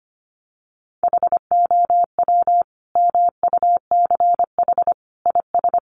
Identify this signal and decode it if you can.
Morse code, heard as a keyed tone, about 25 wpm, 700 Hz. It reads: HOW MUCH IS
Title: Sentences from Top 1100 Words 25wpm